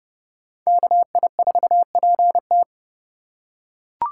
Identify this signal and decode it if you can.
Morse code, heard as a keyed tone, about 28 wpm, 700 Hz. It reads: KI4PT